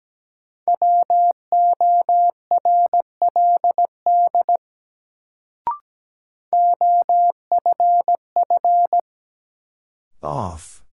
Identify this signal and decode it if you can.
Morse code, heard as a keyed tone, about 17 wpm, 700 Hz. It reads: WORLD OFF
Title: Sentences from Top 200 Words - New Word Drill - 17wpm